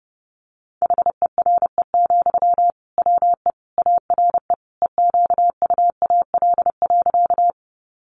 Morse code, heard as a keyed tone, about 30 wpm, 700 Hz. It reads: HERE, WE ARE EQUAL.
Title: Sentences from Top 1400 Words 30wpm